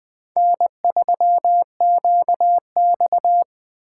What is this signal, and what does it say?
Morse code, heard as a keyed tone, about 23 wpm, 700 Hz. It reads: N3QX
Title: Speed Racing - USA Call Signs 20wpm